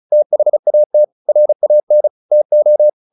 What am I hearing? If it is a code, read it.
Morse code, heard as a keyed tone, about 34 wpm, 600 Hz. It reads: THAT RAN TO
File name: the old man and the sea / the old man and the sea 1 35wpm